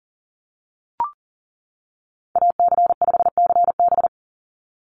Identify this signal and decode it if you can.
Morse code, heard as a keyed tone, about 40 wpm, 690 Hz. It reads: AC5CB
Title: USA Call Signs 40wpm